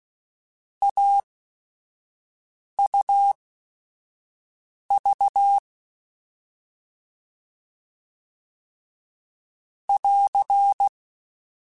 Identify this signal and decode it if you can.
Morse code, heard as a keyed tone, about 16 wpm, spaced as for 5 wpm, 800 Hz. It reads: AUV <AR>
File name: Short Runs / Short Runs-mp3s / AUV AUV4 @5